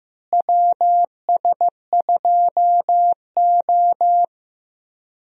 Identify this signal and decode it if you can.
Morse code, heard as a keyed tone, about 15 wpm, 700 Hz. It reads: WS2O
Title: USA Call Signs - 4 Characters - Repeated 3x - 15wpm